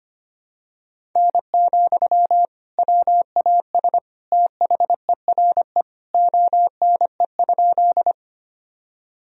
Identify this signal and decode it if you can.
Morse code, heard as a keyed tone, about 25 wpm, 700 Hz. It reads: N, WAS THERE ONE?
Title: Sentences from Top 100 Words and Abbreviations 25wpm